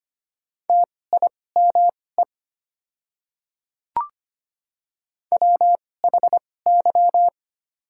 Morse code, heard as a keyed tone, about 25 wpm, spaced as for 18 wpm, 700 Hz. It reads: TIME WHY
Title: 2x Character Spacing - Top 300 Words - 25wpm